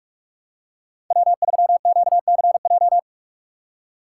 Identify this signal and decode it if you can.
Morse code, heard as a keyed tone, about 46 wpm, 700 Hz. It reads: W3QCJ
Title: USA Call Signs - 5 Characters - 4x Character Spacing - 45wpm